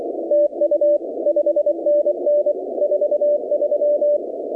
Morse code, heard as a keyed tone, about 24 wpm, 590 Hz, with noise. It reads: TU 5NN 43
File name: contests / 2019-cw-wpx-kt5j-k5pi- -2330 0-17